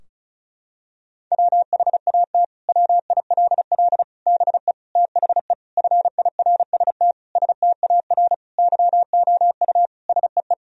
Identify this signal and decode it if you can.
Morse code, heard as a keyed tone, about 36 wpm, 700 Hz. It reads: WHAT WILL BE THE FIRST STAR YOU SEE
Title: Sentences from Top 1000 Words - Review Part 2 - 35wpm